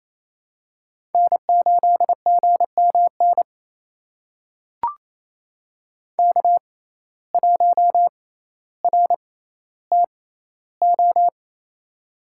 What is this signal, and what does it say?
Morse code, heard as a keyed tone, about 28 wpm, spaced as for 14 wpm, 700 Hz. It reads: N8GMN K1RTO
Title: USA Call Signs - 6x Character Spacing - 28wpm